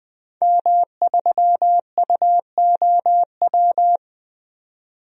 Morse code, heard as a keyed tone, about 20 wpm, 700 Hz. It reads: M3UOW
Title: DXCC-100 Call Signs - 5 Characters - 20wpm